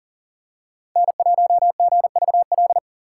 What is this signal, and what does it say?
Morse code, heard as a keyed tone, about 40 wpm, 700 Hz. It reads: N1GVL
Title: USA Call Signs - 5 Characters - 3x Character Spacing - 40wpm